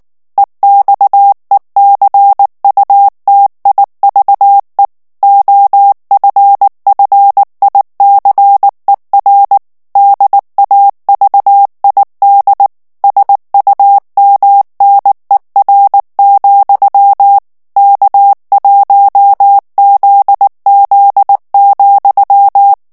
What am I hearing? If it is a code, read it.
Morse code, heard as a keyed tone, about 19 wpm, 800 Hz. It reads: EXECUTIVE OFFICER DAVID SUMNER, K1ZZ,
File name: ANGLAIS texte clair 3 vitesse 18 mots minute